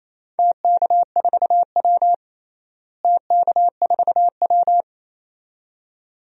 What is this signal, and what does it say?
Morse code, heard as a keyed tone, about 28 wpm, 700 Hz. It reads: TK4W TK4W
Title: DXCC-100 Call Signs - 4 Characters - Repeated 2x - 28wpm